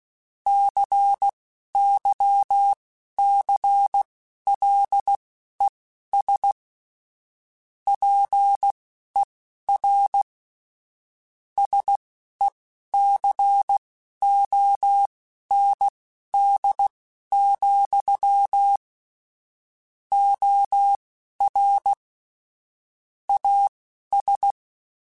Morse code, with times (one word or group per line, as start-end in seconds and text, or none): 0.5-6.5 CYCLES
7.9-10.2 PER
11.6-18.8 SECOND,
20.1-21.9 OR
23.3-24.5 AS